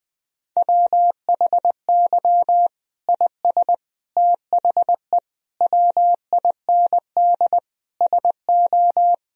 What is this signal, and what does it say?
Morse code, heard as a keyed tone, about 20 wpm, 700 Hz. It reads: WHY IS THE WIND SO
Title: Sentences from Top 1000 Words - Review Part 2 - 20wpm